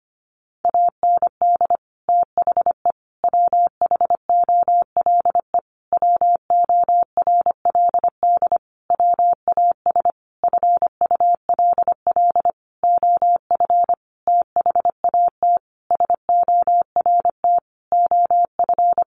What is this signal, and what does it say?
Morse code, heard as a keyed tone, about 25 wpm, 700 Hz. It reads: AND THE WHOLE WORLD WAS FULL OF THAT SORT OF
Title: Sentences from Top 1300 Words 25wpm